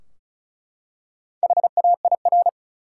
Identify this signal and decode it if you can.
Morse code, heard as a keyed tone, about 36 wpm, 700 Hz. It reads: HAIR
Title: Four Letter Words - 35wpm